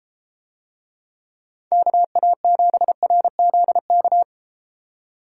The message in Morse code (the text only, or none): KA7RZK